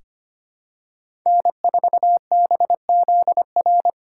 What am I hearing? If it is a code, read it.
Morse code, heard as a keyed tone, about 25 wpm, 700 Hz. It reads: N4BZR
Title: USA Call Signs - 2x Character Spacing - 25wpm